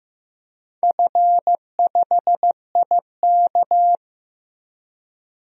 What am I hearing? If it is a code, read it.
Morse code, heard as a keyed tone, about 15 wpm, 700 Hz. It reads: F5IK
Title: DXCC-100 Call Signs - 4 Characters - Repeated 2x - 15wpm